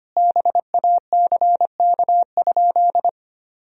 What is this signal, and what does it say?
Morse code, heard as a keyed tone, about 25 wpm, 700 Hz. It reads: BACK?
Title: Sentences from Top 1000 Words - Review Part 4 - 25wpm